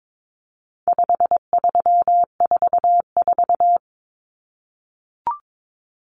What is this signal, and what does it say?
Morse code, heard as a keyed tone, about 22 wpm, 700 Hz. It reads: 5344
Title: Numbers - 4-Digit 22wpm